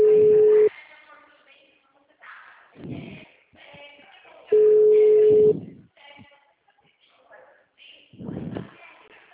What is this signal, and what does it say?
Ringback tone (Europe, 425 Hz), with speech.